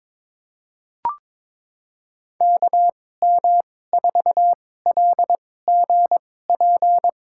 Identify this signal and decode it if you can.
Morse code, heard as a keyed tone, about 22 wpm, spaced as for 16 wpm, 700 Hz. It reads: KM4LGP
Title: USA Call Signs - 2x Character Spacing - 22wpm